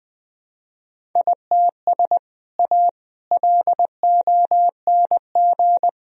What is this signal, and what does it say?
Morse code, heard as a keyed tone, about 20 wpm, 700 Hz. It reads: ITS A LONG
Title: Sentences from Top 1000 Words - Review Part 1 - 20wpm